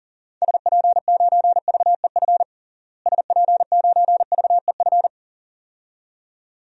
Morse code, heard as a keyed tone, about 37 wpm, 700 Hz. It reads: SP9VEF SP9VEF
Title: DXCC-100 Call Signs - 6 Characters - Repeated 2x - 40wpm